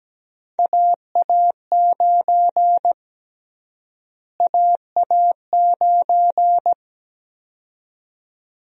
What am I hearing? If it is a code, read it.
Morse code, heard as a keyed tone, about 17 wpm, 700 Hz. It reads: AA9 AA9
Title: USA Call Sign Prefixes - Repeated 2x - 17wpm